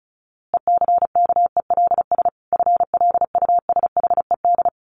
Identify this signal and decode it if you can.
Morse code, heard as a keyed tone, about 34 wpm, 700 Hz. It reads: ECKELS FLUSHED
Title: A Sound Of Thunder 35wpm